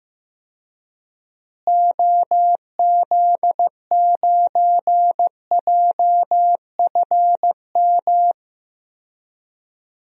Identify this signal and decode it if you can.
Morse code, heard as a keyed tone, about 15 wpm, 700 Hz. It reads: OZ9JFM
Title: DXCC-100 Call Signs - 6 Characters - Repeated 2x - 15wpm